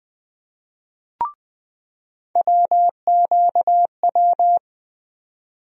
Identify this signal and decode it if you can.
Morse code, heard as a keyed tone, about 20 wpm, 700 Hz. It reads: WQW
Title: USA Call Sign Postfixes - 3 Letters 20wpm